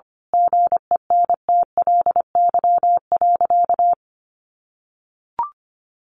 Morse code, heard as a keyed tone, about 25 wpm, 700 Hz. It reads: GENTLY.
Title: Sentences from Top 1500 Words 25wpm